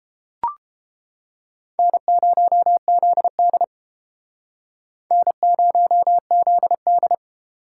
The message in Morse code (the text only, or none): N0ZD N0ZD